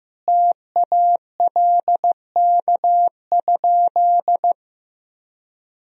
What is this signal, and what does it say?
Morse code, heard as a keyed tone, about 15 wpm, 700 Hz. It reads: TALK?
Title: Sentences from Top 1000 Words - Review Part 1 - 15wpm